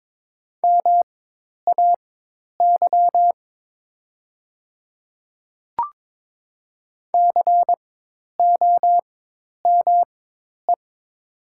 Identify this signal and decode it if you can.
Morse code, heard as a keyed tone, about 22 wpm, spaced as for 10 wpm, 700 Hz. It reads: MAY COME
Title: 4x Character Spacing - Top 100 Words - 22wpm